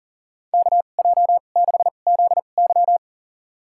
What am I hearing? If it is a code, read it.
Morse code, heard as a keyed tone, about 40 wpm, spaced as for 29 wpm, 700 Hz. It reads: KJ6ZY
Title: USA Call Signs - 5 Characters - 2x Character Spacing - 40wpm